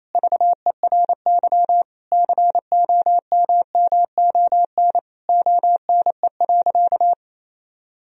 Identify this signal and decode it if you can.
Morse code, heard as a keyed tone, about 28 wpm, 700 Hz. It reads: VERY COMMON ONE.
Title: Sentences from Top 1000 Words - Review Part 4 - 28wpm